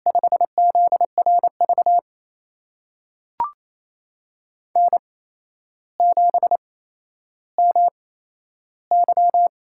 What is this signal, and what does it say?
Morse code, heard as a keyed tone, about 28 wpm, spaced as for 12 wpm, 700 Hz. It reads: 5ZRV N7MY
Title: USA Call Signs - 8x Character Spacing - 28wpm